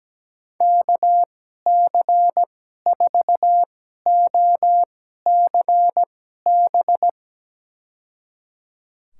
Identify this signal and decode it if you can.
Morse code, heard as a keyed tone, about 17 wpm, spaced as for 12 wpm, 700 Hz. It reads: KC4OCB
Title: USA Call Signs - 2x Character Spacing - 17wpm